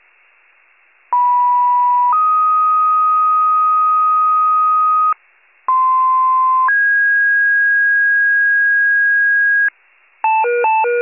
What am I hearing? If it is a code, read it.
Two-tone page, heard as a one-second tone, then a three-second tone, about 980 then 1250 Hz, about 1040 then 1650 Hz, with noise. It is heard from 1.1 to 9.7 s.
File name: wcalert